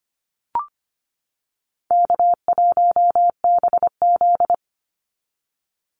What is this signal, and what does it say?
Morse code, heard as a keyed tone, about 25 wpm, 700 Hz. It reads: K1BZ
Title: Call Signs - 25wpm